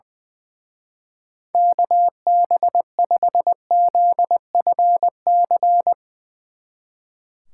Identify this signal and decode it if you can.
Morse code, heard as a keyed tone, about 20 wpm, 700 Hz. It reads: KB5ZFC
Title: Speed Racing - USA 2x3 Call Signs 20wpm